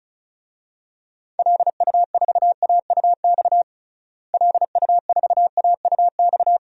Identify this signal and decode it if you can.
Morse code, heard as a keyed tone, about 36 wpm, 700 Hz. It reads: LU4AUX LU4AUX
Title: DXCC-100 Call Signs - 6 Characters - Repeated 2x - 35wpm